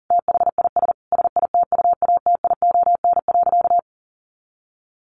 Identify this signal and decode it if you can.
Morse code, heard as a keyed tone, about 37 wpm, 700 Hz. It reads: THIS SITUATION.
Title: Sentences from Top 1200 Words 40wpm